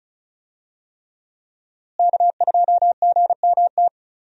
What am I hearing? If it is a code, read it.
Morse code, heard as a keyed tone, about 35 wpm, 700 Hz. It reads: K2GMT
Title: USA Call Signs - 5 Characters - Repeated 3x - 35wpm